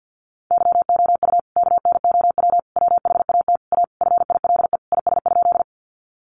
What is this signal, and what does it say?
Morse code, heard as a keyed tone, about 50 wpm, 700 Hz. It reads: YOU KNOW WHAT A FILE IS?
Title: Sentences from Top 1200 Words 50wpm